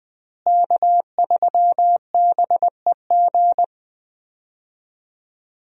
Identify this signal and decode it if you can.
Morse code, heard as a keyed tone, about 20 wpm, 700 Hz. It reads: K3BEG
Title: USA Call Signs - 5 Characters - Repeated 2x - 20wpm